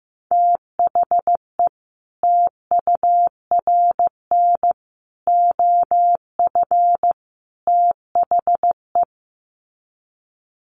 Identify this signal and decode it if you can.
Morse code, heard as a keyed tone, about 15 wpm, 700 Hz. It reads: THE TURN OF THE
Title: Sets of 4 Words 15wpm